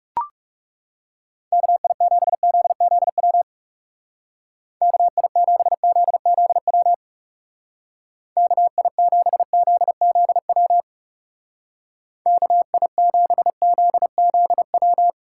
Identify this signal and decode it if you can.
Morse code, heard as a keyed tone, about 35 wpm, 700 Hz. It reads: KI7ZZW KI7ZZW KI7ZZW KI7ZZW
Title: Speed Racing - USA Call Signs 30wpm